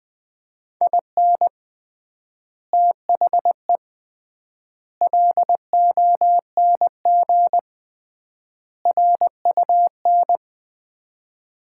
Morse code, heard as a keyed tone, about 21 wpm, 700 Hz. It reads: IN THE LONG RUN
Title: 3x Word-Spacing - Sets of 4 Words - 20wpm